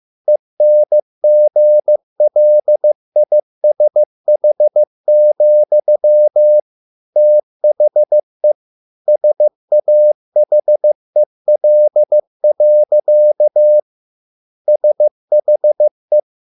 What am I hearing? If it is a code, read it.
Morse code, heard as a keyed tone, about 15 wpm, 600 Hz. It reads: ENGLISH, THE SAHEL. SHE